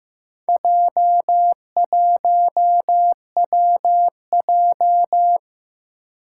Morse code, heard as a keyed tone, about 15 wpm, 700 Hz. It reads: J1WJ
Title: Four Challenging Letter-Number Combinations 15wpm